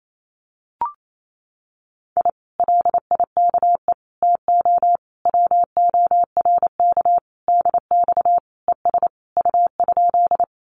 Morse code, heard as a keyed tone, about 28 wpm, 700 Hz. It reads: I LIKE TO WORK DX ES U?
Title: Common QSO Phrases Encore 28wpm